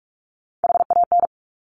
Morse code, heard as a keyed tone, about 46 wpm, 700 Hz. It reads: HAN
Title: Top Three Letter Combinations in Words 45wpm